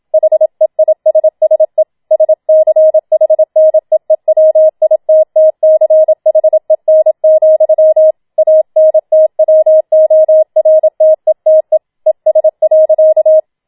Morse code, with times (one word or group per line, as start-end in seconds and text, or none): 0.1-1.8 HEISSE
2.1-8.1 SCHNEEWITTCHEN,
8.4-11.8 ANTWORTETE
12.1-13.4 ES.